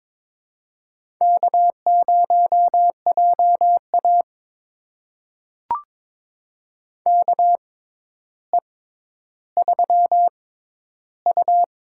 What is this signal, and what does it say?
Morse code, heard as a keyed tone, about 22 wpm, spaced as for 11 wpm, 700 Hz. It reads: K0JA KE3U
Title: USA Call Signs - 4 Characters - 6x Character Spacing - 22wpm